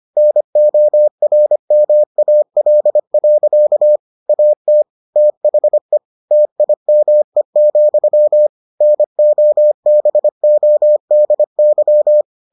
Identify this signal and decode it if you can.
Morse code, heard as a keyed tone, about 25 wpm, 600 Hz. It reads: NORMAL. AT THE TIME, NOBODY